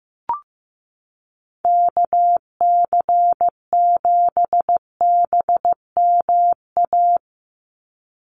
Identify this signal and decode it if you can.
Morse code, heard as a keyed tone, about 15 wpm, 700 Hz. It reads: KC7BMA
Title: USA Call Signs 15wpm